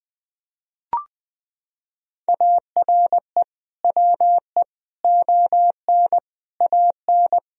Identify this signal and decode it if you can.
Morse code, heard as a keyed tone, about 20 wpm, 700 Hz. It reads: ARE WE ON AN
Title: Sentences from Top 1000 Words - Review Part 3 - 20wpm